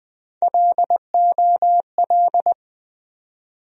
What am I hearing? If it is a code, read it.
Morse code, heard as a keyed tone, about 20 wpm, 700 Hz. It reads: LOL
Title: USA Call Sign Postfixes - 3 Letters 20wpm